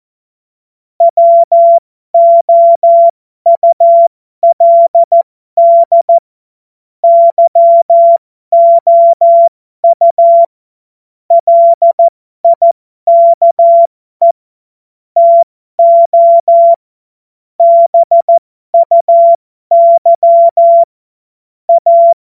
Morse code, heard as a keyed tone, about 14 wpm, spaced as for 12 wpm, 680 Hz. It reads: WOULD YOU LIKE TO BUY A
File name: Morsekurs / Witze